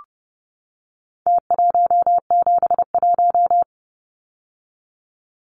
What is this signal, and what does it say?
Morse code, heard as a keyed tone, about 30 wpm, 700 Hz. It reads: T171
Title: Numbers - Contest Style - 4-Digit 30wpm